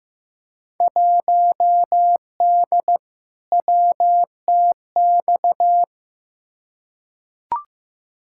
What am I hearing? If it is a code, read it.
Morse code, heard as a keyed tone, about 15 wpm, 700 Hz. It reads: 1D WTX
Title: Field Day Exchanges - 15wpm